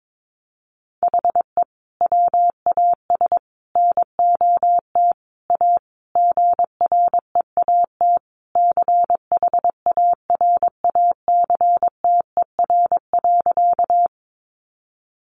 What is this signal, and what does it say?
Morse code, heard as a keyed tone, about 22 wpm, 700 Hz. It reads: HE WAS NOT A GREAT CHARACTER.
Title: Sentences from Top 1200 Words 22wpm